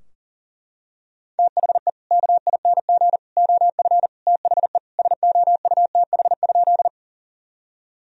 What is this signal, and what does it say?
Morse code, heard as a keyed tone, about 40 wpm, 700 Hz. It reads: THE KING OF THE SOUTH?
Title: Sentences from Top 1000 Words - Review Part 2 - 40wpm